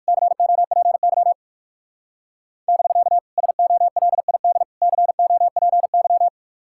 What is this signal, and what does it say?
Morse code, heard as a keyed tone, about 58 wpm, 700 Hz. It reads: COPY <BK> SOLID COPY